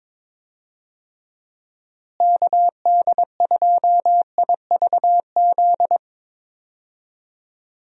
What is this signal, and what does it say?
Morse code, heard as a keyed tone, about 22 wpm, 700 Hz. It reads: KD2IVZ
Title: USA Call Signs - 6 Characters - Repeated 2x - 22wpm